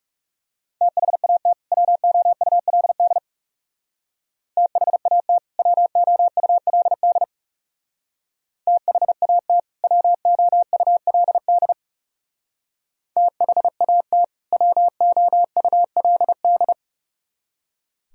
Morse code, heard as a keyed tone, about 37 wpm, 700 Hz. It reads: THAT WOULD THAT WOULD THAT WOULD THAT WOULD
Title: Speed-Racing - Sets of 2 Words 30wpm